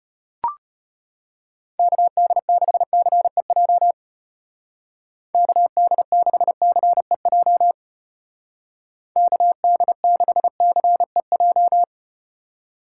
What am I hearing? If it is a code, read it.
Morse code, heard as a keyed tone, about 34 wpm, 700 Hz. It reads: KD6CEJ KD6CEJ KD6CEJ